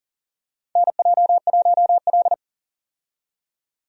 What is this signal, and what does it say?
Morse code, heard as a keyed tone, about 40 wpm, 700 Hz. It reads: NJ1L
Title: USA Call Signs - 4 Characters - 40wpm